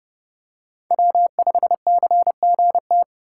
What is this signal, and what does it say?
Morse code, heard as a keyed tone, about 30 wpm, 700 Hz. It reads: W5CGT
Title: USA Call Signs - 5 Characters - 4x Character Spacing - 30wpm